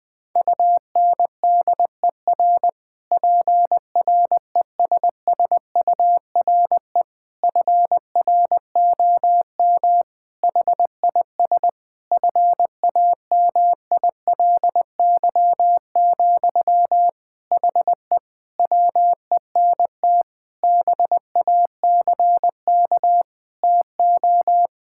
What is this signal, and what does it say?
Morse code, heard as a keyed tone, about 21 wpm, 700 Hz. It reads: UNDER PRESSURE FROM HIS FAMILY, HE WENT BACK TO
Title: Sentences from Top 1000 Words - Review Part 6 - 20wpm